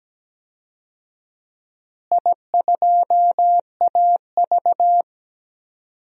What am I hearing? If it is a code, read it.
Morse code, heard as a keyed tone, about 17 wpm, 700 Hz. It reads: I2AV
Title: DXCC-100 Call Signs - 4 Characters - Repeated 2x - 17wpm